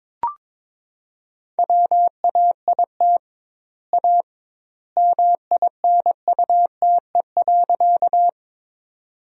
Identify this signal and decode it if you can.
Morse code, heard as a keyed tone, about 21 wpm, 700 Hz. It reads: WAIT A MINUTE.
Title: 2x Word-Spacing - Sentences from Top 500 Words - 22wpm